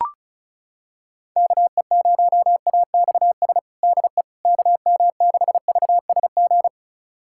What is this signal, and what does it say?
Morse code, heard as a keyed tone, about 34 wpm, 700 Hz. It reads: KE0AXS DE KM6VSG
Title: USA Callsign DE Callsign 35wpm